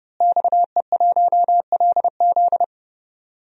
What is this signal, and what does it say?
Morse code, heard as a keyed tone, about 30 wpm, 700 Hz. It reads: XE1LZ
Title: DXCC-100 Call Signs - 5 Characters - Repeated 2x - 30wpm